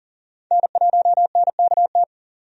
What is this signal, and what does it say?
Morse code, heard as a keyed tone, about 40 wpm, 700 Hz. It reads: N1NKT
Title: USA Call Signs - 5 Characters - 40wpm